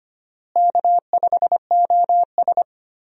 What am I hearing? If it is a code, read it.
Morse code, heard as a keyed tone, about 25 wpm, 700 Hz. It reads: K5OS
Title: USA Call Signs - 4 Characters - Repeated 3x - 25wpm